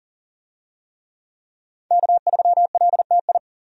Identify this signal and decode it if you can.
Morse code, heard as a keyed tone, about 40 wpm, 700 Hz. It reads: K3LTI
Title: USA Call Signs - 5 Characters - 40wpm